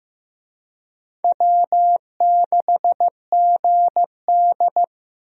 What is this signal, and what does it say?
Morse code, heard as a keyed tone, about 15 wpm, 700 Hz. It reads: W6GD
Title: USA Call Signs - 4 Characters - Repeated 2x - 15wpm